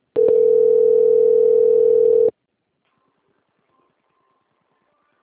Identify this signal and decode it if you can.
Ringback tone (North America), 440 + 480 Hz.